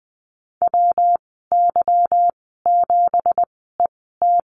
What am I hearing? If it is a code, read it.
Morse code, heard as a keyed tone, about 20 wpm, spaced as for 14 wpm, 700 Hz. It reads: WY7ET
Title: 167 Lesson 27g - Number 7 - Call Signs 20wpm